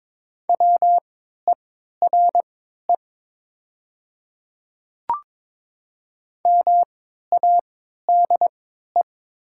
Morse code, heard as a keyed tone, about 23 wpm, spaced as for 13 wpm, 700 Hz. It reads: WERE MADE
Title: 3x Character Spacing - Top 200 Words - 22wpm